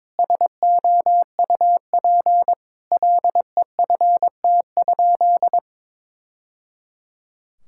Morse code, heard as a keyed tone, about 22 wpm, 700 Hz. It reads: SOUP LEFT?
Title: Sentences from Top 1000 Words - Review Part 6 - 22wpm